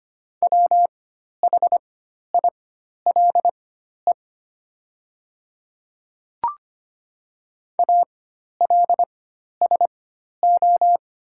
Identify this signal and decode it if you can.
Morse code, heard as a keyed tone, about 25 wpm, spaced as for 12 wpm, 700 Hz. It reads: WHILE ALSO
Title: 4x Character Spacing - Top 300 Words - 25wpm